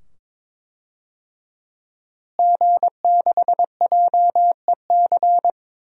Morse code, heard as a keyed tone, about 21 wpm, 700 Hz. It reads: G6JEC
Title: DXCC-100 Call Signs - 5 Characters - Repeated 2x - 22wpm